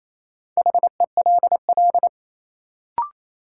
Morse code, heard as a keyed tone, about 27 wpm, 700 Hz. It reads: HELL
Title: Sets of 5 Words Encore - New Word Drill - 28wpm